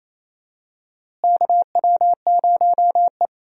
Morse code, heard as a keyed tone, about 28 wpm, 700 Hz. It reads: KW0E
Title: USA Call Signs - 4 Characters - Repeated 3x - 28wpm